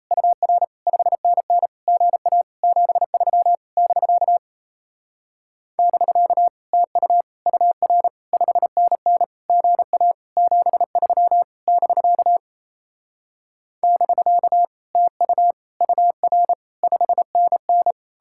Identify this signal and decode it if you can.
Morse code, heard as a keyed tone, about 38 wpm, 700 Hz. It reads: UR 5NN GA 73 <BK> <BK> TU UR 5NN GA 73 <BK> <BK> TU UR 5NN